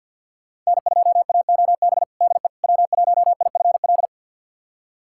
Morse code, heard as a keyed tone, about 52 wpm, 700 Hz. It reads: N2AOB DE W1IFL
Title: USA Callsign DE Callsign - 5 Characters - 50wpm